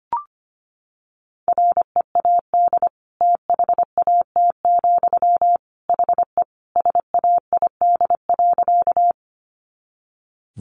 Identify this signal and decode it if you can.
Morse code, heard as a keyed tone, about 25 wpm, 700 Hz. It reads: READ THAT, HE SAID.
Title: Sentences from Top 200 Words 25wpm